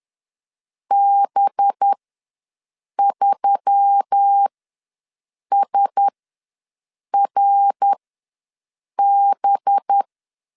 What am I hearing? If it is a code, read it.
Morse code, heard as a keyed tone, about 10 wpm, spaced as for 6 wpm, 800 Hz. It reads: B3SRB
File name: LEZIONI / 5 LBVC / 5 LB ws 10wpm 3616